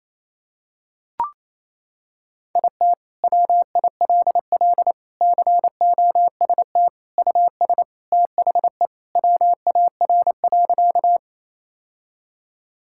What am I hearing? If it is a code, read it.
Morse code, heard as a keyed tone, about 27 wpm, 700 Hz. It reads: IT WILL COST US THE WAR.
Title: Sentences from Top 1000 Words - Review Part 5 - 28wpm